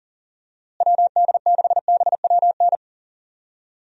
Morse code, heard as a keyed tone, about 40 wpm, 700 Hz. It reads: WD6BWN
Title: USA Call Signs - 6 Characters - Repeated 2x - 40wpm